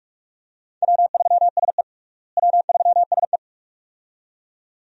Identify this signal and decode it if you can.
Morse code, heard as a keyed tone, about 46 wpm, 700 Hz. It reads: W3SE W3SE
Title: USA Call Signs - 4 Characters - Repeated 2x - 45wpm